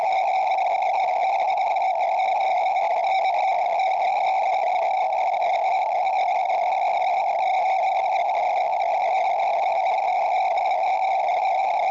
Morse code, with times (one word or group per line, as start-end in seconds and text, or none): none